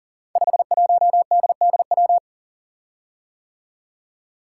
Morse code, heard as a keyed tone, about 38 wpm, 700 Hz. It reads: 51DDW
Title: Code Groups - 5 Letter:Numbers - 40wpm